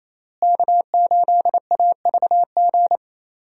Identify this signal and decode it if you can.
Morse code, heard as a keyed tone, about 28 wpm, 700 Hz. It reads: K8AVG
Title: USA Call Signs - 5 Characters - Repeated 3x - 28wpm